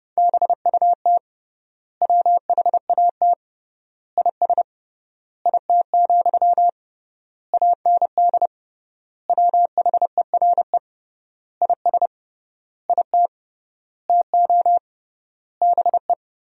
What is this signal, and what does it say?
Morse code, heard as a keyed tone, about 30 wpm, 700 Hz. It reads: BUT WHAT IS IT, AND WHERE IS IT TO BE
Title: 3x Word-Spacing - Sentences from Top 300 Words - 30wpm